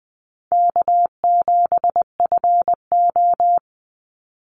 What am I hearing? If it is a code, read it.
Morse code, heard as a keyed tone, about 20 wpm, 700 Hz. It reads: K7FO
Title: NA SOTA Chaser Callsigns 20wpm